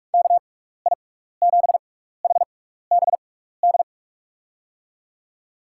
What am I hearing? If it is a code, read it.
Morse code, heard as a keyed tone, about 44 wpm, spaced as for 15 wpm, 700 Hz. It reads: KI7HBD
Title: USA Call Signs - 6 Characters - 6x Character Spacing - 45wpm